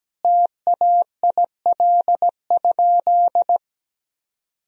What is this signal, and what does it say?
Morse code, heard as a keyed tone, about 17 wpm, 700 Hz. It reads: TAIL?
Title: Sentences from Top 1000 Words - Review Part 3 - 17wpm